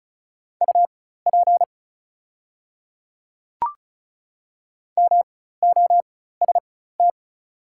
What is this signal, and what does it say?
Morse code, heard as a keyed tone, about 34 wpm, spaced as for 16 wpm, 700 Hz. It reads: UP MOST
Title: 4x Character Spacing - Top 100 Words - 35wpm